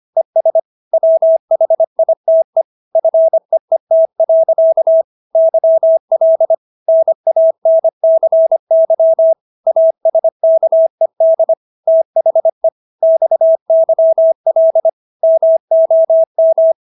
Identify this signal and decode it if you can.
Morse code, heard as a keyed tone, about 25 wpm, 650 Hz. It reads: ES WHITE FEET. YL NANCY ASKED THE XYL MOM